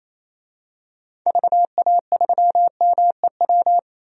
Morse code, heard as a keyed tone, about 28 wpm, 700 Hz. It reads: VA3MEW
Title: DXCC-100 Call Signs - 6 Characters - Repeated 3x - 28wpm